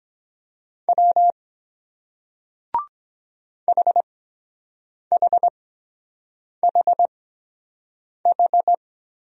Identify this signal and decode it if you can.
Morse code, heard as a keyed tone, about 23 wpm, spaced as for 7 wpm, 700 Hz. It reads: W HHHH